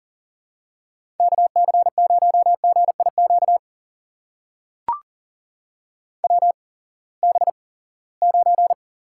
Morse code, heard as a keyed tone, about 40 wpm, spaced as for 23 wpm, 700 Hz. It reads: KC0GIQ WB9
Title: USA Call Signs - 6 Characters - 8x Character Spacing - 40wpm